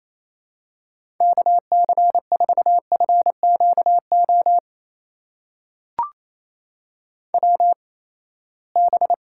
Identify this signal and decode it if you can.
Morse code, heard as a keyed tone, about 28 wpm, spaced as for 19 wpm, 700 Hz. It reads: KC4FQO WB
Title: USA Call Signs - 6 Characters - 8x Character Spacing - 28wpm